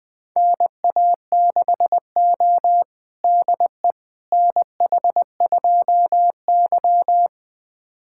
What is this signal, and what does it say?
Morse code, heard as a keyed tone, about 21 wpm, 700 Hz. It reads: NA6O DE NH2Y
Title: USA Callsign DE Callsign - 4 Characters - 20wpm